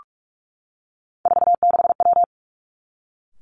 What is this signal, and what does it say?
Morse code, heard as a keyed tone, about 46 wpm, 700 Hz. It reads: 46W
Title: Three Challenging Letter-Number Combinations 45wpm